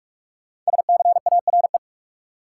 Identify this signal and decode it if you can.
Morse code, heard as a keyed tone, about 46 wpm, 700 Hz. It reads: SCARE
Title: Sentences from Top 1000 Words - Review Part 5 - New Word Drill - 45wpm